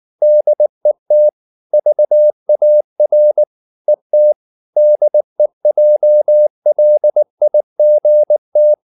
Morse code, heard as a keyed tone, about 19 wpm, 600 Hz. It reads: DET VAR ET DEJLIGT
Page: Ferie 19 wpm